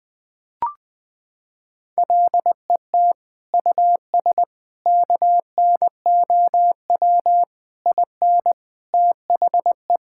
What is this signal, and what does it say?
Morse code, heard as a keyed tone, about 20 wpm, 700 Hz. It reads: LET US KNOW IN THE
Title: Sets of 6 Words - 20wpm